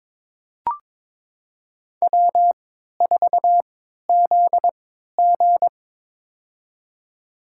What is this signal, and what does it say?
Morse code, heard as a keyed tone, about 22 wpm, spaced as for 12 wpm, 700 Hz. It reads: W4ZG
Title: USA Call Signs - 3x Character Spacing - 22wpm